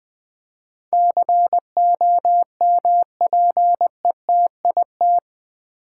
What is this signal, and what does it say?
Morse code, heard as a keyed tone, about 20 wpm, 700 Hz. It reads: COMPETIT
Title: Build A Word - 4 Letters At A Time - 10-Letter Words - 20wpm